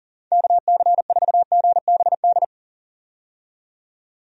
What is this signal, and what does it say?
Morse code, heard as a keyed tone, about 40 wpm, 700 Hz. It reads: KC4GBD
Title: USA Call Signs - 6 Characters - 40wpm